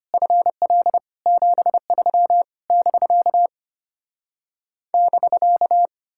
Morse code, heard as a keyed tone, about 33 wpm, 700 Hz. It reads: FL 73 <BK> <BK>